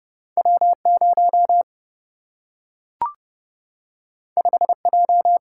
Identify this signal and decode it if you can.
Morse code, heard as a keyed tone, about 30 wpm, 700 Hz. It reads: W0 5J
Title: Single-Letter Single-Number 30wpm